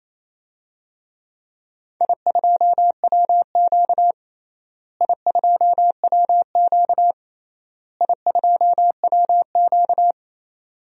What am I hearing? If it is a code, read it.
Morse code, heard as a keyed tone, about 28 wpm, 700 Hz. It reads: I2WQ I2WQ I2WQ
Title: DXCC-100 Call Signs - 4 Characters - Repeated 3x - 28wpm